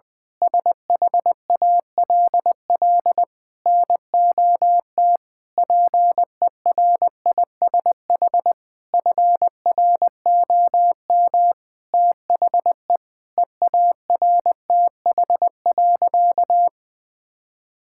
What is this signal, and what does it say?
Morse code, heard as a keyed tone, about 20 wpm, 700 Hz. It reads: SHALL NOT PERISH FROM THE EARTH.
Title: Sentences with 9-Word Phrases - 20wpm